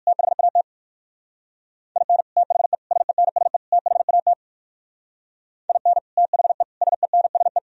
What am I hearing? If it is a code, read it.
Morse code, heard as a keyed tone, about 51 wpm, 700 Hz. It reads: THAT IN THE SENSE THAT IN THE SENSE
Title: Speed-Racing - Sets of 4 Words 45wpm